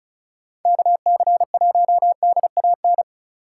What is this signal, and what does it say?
Morse code, heard as a keyed tone, about 43 wpm, 700 Hz. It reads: KC1DAN
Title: Speed Racing - USA Call Signs 35wpm